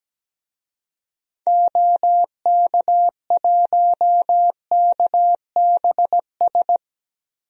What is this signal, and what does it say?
Morse code, heard as a keyed tone, about 17 wpm, 700 Hz. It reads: OK1KBS
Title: DXCC-100 Call Signs - 6 Characters - Repeated 2x - 17wpm